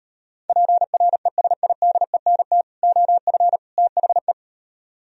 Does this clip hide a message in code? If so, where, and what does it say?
Yes: Morse code from 0.5 to 4.3 s, heard as a keyed tone, about 38 wpm, 700 Hz. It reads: PRESIDENT OF THE